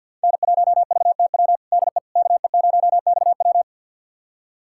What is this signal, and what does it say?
Morse code, heard as a keyed tone, about 53 wpm, 700 Hz. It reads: N1VTW DE KE0XW
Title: USA Callsign DE Callsign - 5 Characters - 50wpm